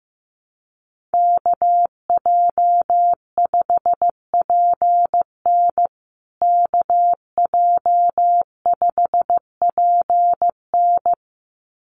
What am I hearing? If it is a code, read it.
Morse code, heard as a keyed tone, about 15 wpm, 700 Hz. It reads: KJ5PN KJ5PN
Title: USA Call Signs Repeated 2x 15wpm